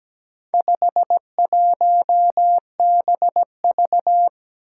Morse code, heard as a keyed tone, about 17 wpm, 700 Hz. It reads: 51BV
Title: Four Challenging Letter-Number Combinations 17wpm